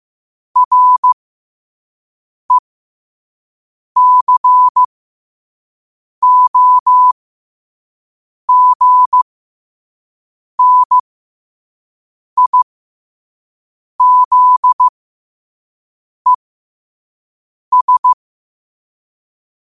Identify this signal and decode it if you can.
Morse code, heard as a keyed tone, about 15 wpm, spaced as for 5 wpm, 1000 Hz. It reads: RECOGNIZES
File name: qotdfiles / 5wpm